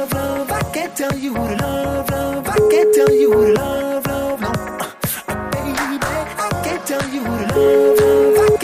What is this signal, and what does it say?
Ringback tone (Europe, 425 Hz), with noise.